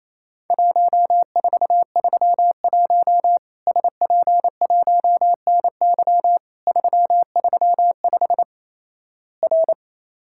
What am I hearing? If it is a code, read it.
Morse code, heard as a keyed tone, about 28 wpm, 700 Hz. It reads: 1431 SP1NY 335 R
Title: Worked All Europe DX Contest - QTC Exchange - 28wpm